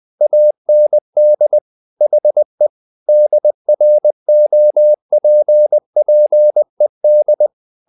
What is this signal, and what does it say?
Morse code, heard as a keyed tone, about 20 wpm, 600 Hz. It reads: AND HE DROPPED
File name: the old man and the sea / the old man and the sea part 1 20wpm